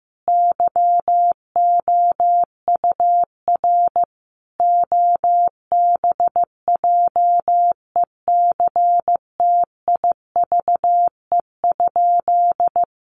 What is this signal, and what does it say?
Morse code, heard as a keyed tone, about 15 wpm, 700 Hz. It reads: YOUR OBJECTIVE?
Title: Sentences from Top 1200 Words 15wpm